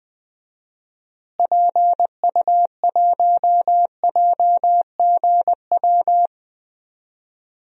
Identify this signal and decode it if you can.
Morse code, heard as a keyed tone, about 20 wpm, 700 Hz. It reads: PU1JGW
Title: DXCC-100 Call Signs - 6 Characters - Repeated 2x - 20wpm